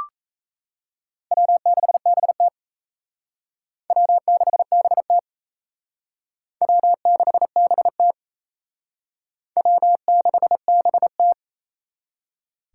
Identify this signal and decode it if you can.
Morse code, heard as a keyed tone, about 36 wpm, 700 Hz. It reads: W6BT W6BT W6BT W6BT